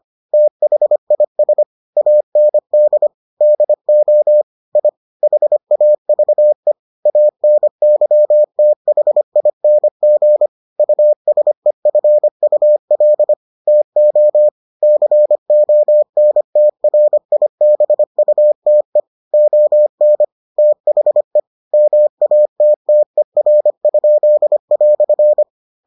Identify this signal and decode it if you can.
Morse code, heard as a keyed tone, about 25 wpm, 600 Hz. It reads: THIS AND DO I HAVE ANYTHING USEFUL TO CONTRIBUTE ON THE MATTER?"